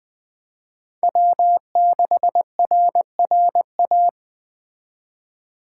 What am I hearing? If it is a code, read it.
Morse code, heard as a keyed tone, about 20 wpm, 700 Hz. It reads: W6RRA
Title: Speed Racing - USA 1x3 Call Signs 17wpm